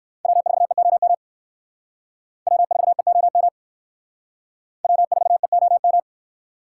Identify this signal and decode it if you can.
Morse code, heard as a keyed tone, about 64 wpm, 700 Hz. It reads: W4EOM W4EOM W4EOM